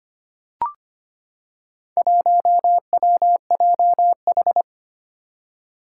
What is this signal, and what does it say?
Morse code, heard as a keyed tone, about 25 wpm, 700 Hz. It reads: 1WJH
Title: Four Challenging Letter-Number Combinations 25wpm